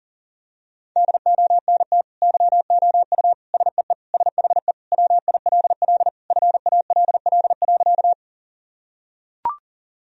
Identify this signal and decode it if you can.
Morse code, heard as a keyed tone, about 40 wpm, 700 Hz. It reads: DONT YOU SEE SHE WILL FALL.
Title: Sentences from Top 1000 Words - Review Part 2 - 40wpm